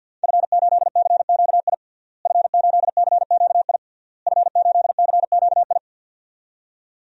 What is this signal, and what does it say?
Morse code, heard as a keyed tone, about 47 wpm, 700 Hz. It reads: F8CQI F8CQI F8CQI
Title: DXCC-100 Call Signs - 5 Characters - Repeated 3x - 50wpm